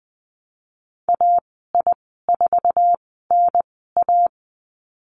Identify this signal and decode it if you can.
Morse code, heard as a keyed tone, about 20 wpm, spaced as for 15 wpm, 700 Hz. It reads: AI4NA
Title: 179 Lesson 29g - Slash - Call Signs 20wpm